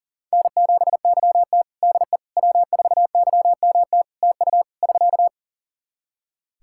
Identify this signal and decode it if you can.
Morse code, heard as a keyed tone, about 40 wpm, 700 Hz. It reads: N7YT DE W4YMT TU <SK>
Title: K1USN SST QSO Practice 40wpm